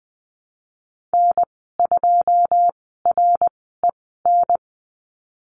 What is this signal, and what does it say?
Morse code, heard as a keyed tone, about 20 wpm, spaced as for 15 wpm, 700 Hz. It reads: N2REN
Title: 088 Lesson 15g - Number 5 - Call Signs 20wpm